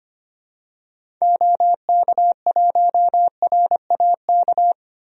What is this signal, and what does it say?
Morse code, heard as a keyed tone, about 25 wpm, 700 Hz. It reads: OK1RAK
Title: DXCC-100 Call Signs - 6 Characters - Repeated 3x - 25wpm